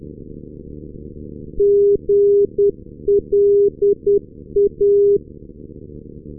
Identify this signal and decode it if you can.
Morse code, heard as a keyed tone, about 10 wpm, 400 Hz. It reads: GLA
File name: europe / sui / sui-48-ndb gland gla-375 2021-10 rb